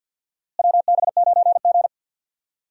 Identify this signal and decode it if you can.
Morse code, heard as a keyed tone, about 53 wpm, 700 Hz. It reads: WB9G
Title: USA Call Signs - 4 Characters - Repeated 2x - 50wpm